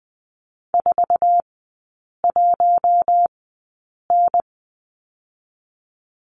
Morse code, heard as a keyed tone, about 20 wpm, spaced as for 8 wpm, 700 Hz. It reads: 41N
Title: 047 Lesson 9e - Number 4 - Three Characters 20wpm